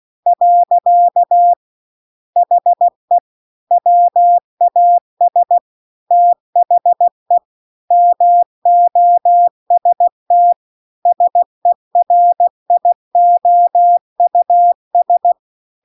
Morse code, heard as a keyed tone, about 16 wpm, 700 Hz. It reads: . HE WAS THE MOST SERIOUS